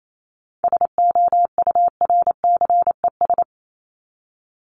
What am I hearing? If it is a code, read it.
Morse code, heard as a keyed tone, about 28 wpm, 700 Hz. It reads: SOURCES
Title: ES:S Suffix - Top 100 Words with Base Words 28wpm